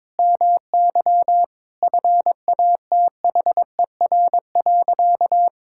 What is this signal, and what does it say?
Morse code, heard as a keyed tone, about 22 wpm, 700 Hz. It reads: MY FATHER.
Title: Sentences from Top 1000 Words - Review Part 5 - 22wpm